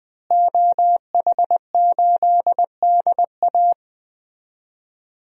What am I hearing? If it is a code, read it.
Morse code, heard as a keyed tone, about 20 wpm, 700 Hz. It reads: OH8DA
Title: DXCC-100 Call Signs - 5 Characters - Repeated 2x - 20wpm